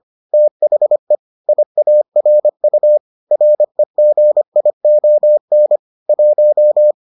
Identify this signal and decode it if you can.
Morse code, heard as a keyed tone, about 25 wpm, 600 Hz. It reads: THE IARU REGION 1